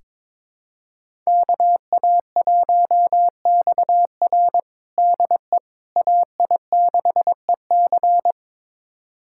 Morse code, heard as a keyed tone, about 23 wpm, 700 Hz. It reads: KA1XR DE AI6EC
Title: USA Callsign DE Callsign - 5 Characters - 22wpm